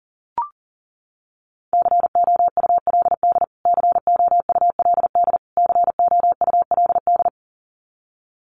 Morse code, heard as a keyed tone, about 40 wpm, 700 Hz. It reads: COULD COULD COULD
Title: Top 200 Words Repeated 40wpm